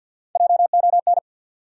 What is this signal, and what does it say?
Morse code, heard as a keyed tone, about 52 wpm, 700 Hz. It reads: JON
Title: USA Common Names - 50wpm